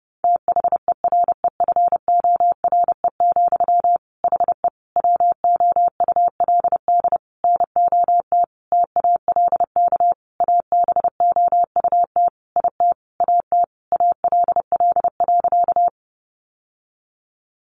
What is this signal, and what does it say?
Morse code, heard as a keyed tone, about 30 wpm, 700 Hz. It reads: THEREFORE, HE WOULD NOT TALK ABOUT IT AT ALL.
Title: Sentences from Top 800 Words 30wpm